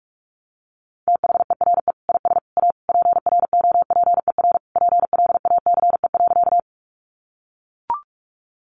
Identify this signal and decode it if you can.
Morse code, heard as a keyed tone, about 44 wpm, 700 Hz. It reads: THERE IS A PROPER PLACE.
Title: Sentences from Top 800 Words 45wpm